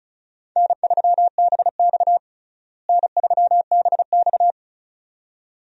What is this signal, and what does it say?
Morse code, heard as a keyed tone, about 36 wpm, 700 Hz. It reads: N3BX N3BX
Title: USA Call Signs - 4 Characters - Repeated 2x - 35wpm